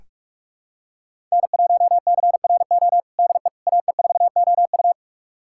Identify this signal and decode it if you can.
Morse code, heard as a keyed tone, about 41 wpm, 700 Hz. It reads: N1CRO DE AE4OU
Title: USA Callsign DE Callsign - 5 Characters - 45wpm